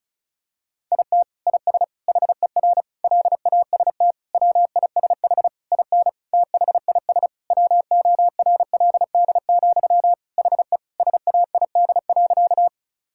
Morse code, heard as a keyed tone, about 34 wpm, 700 Hz. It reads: IT IS HER LAST WISH IN THIS WORLD, HE SAID.
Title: Sentences from Top 1600 Words 35wpm